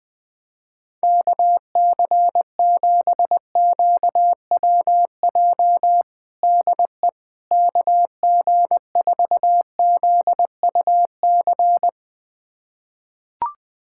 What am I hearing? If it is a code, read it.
Morse code, heard as a keyed tone, about 20 wpm, 700 Hz. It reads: KC7QWJ DE KG4ZUC
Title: USA Callsign DE Callsign - 6 Characters - 20wpm